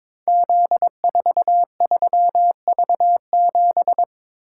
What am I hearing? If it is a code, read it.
Morse code, heard as a keyed tone, about 22 wpm, 700 Hz. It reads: Z43V7
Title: Code Groups - 5 Letter:Numbers - 22wpm